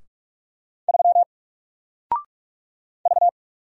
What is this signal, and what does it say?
Morse code, heard as a keyed tone, about 44 wpm, 700 Hz. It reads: 3 V
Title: Single Letter-Number 45wpm